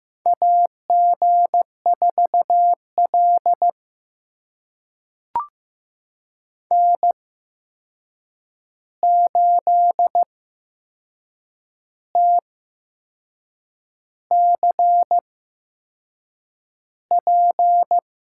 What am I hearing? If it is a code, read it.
Morse code, heard as a keyed tone, about 15 wpm, spaced as for 6 wpm, 700 Hz. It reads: AG4L N8TCP
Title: USA Call Signs - 8x Character Spacing - 15wpm